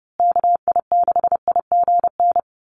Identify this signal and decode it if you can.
Morse code, heard as a keyed tone, about 30 wpm, 700 Hz. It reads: KI6IGN
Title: USA Call Signs Repeated 2x 30wpm